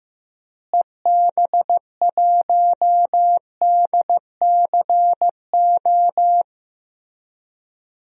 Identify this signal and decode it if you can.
Morse code, heard as a keyed tone, about 15 wpm, 700 Hz. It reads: EB1DCO
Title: DXCC-100 Call Signs - 6 Characters - Repeated 3x - 15wpm